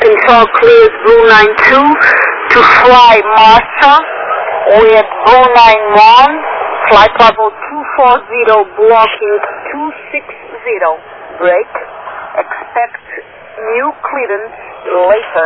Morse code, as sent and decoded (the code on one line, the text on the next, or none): none